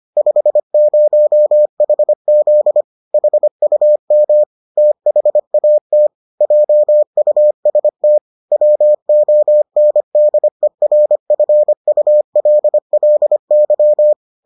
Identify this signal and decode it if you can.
Morse code, heard as a keyed tone, about 25 wpm, 600 Hz. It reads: 50HZ HUM THAT JUST WONDERFULLY